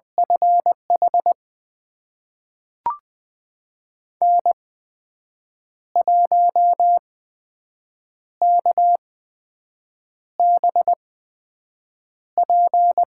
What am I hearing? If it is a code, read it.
Morse code, heard as a keyed tone, about 20 wpm, spaced as for 6 wpm, 700 Hz. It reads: FH N1KBP
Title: USA Call Signs - 5 Characters - 8x Character Spacing - 20wpm